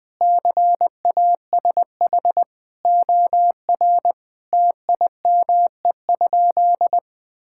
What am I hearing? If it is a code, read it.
Morse code, heard as a keyed tone, about 20 wpm, 700 Hz. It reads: CASH OR TIME?
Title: Sentences from Top 1000 Words - Review Part 5 - 20wpm